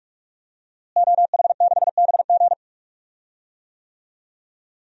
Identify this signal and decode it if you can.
Morse code, heard as a keyed tone, about 48 wpm, 700 Hz. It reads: OH6BG
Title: DXCC-100 Call Signs - 5 Characters - Repeated 2x - 45wpm